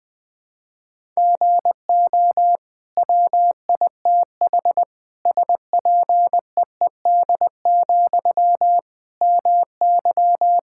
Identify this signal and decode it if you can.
Morse code, heard as a keyed tone, about 21 wpm, 700 Hz. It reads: GO WITH SPEED, MY
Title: Sentences from Top 1000 Words - Review Part 5 - 20wpm